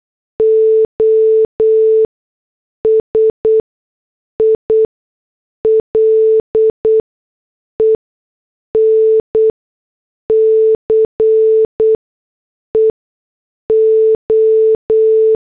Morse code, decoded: O SILENCE O